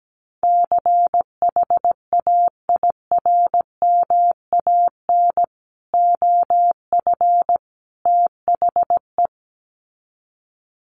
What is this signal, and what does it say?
Morse code, heard as a keyed tone, about 17 wpm, 700 Hz. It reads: CHAIRMAN OF THE
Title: Sets of 3 Words 17wpm